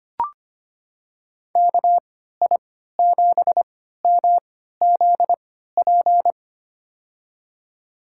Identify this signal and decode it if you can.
Morse code, heard as a keyed tone, about 25 wpm, spaced as for 14 wpm, 700 Hz. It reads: KI7MZP
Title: USA Call Signs - 6 Characters - 3x Character Spacing - 25wpm